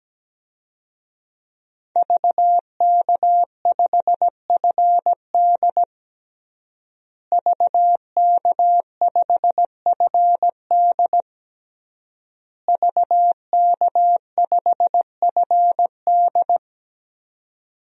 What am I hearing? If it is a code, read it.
Morse code, heard as a keyed tone, about 17 wpm, 700 Hz. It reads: VK5FD VK5FD VK5FD
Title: DXCC-100 Call Signs - 5 Characters - Repeated 3x - 17wpm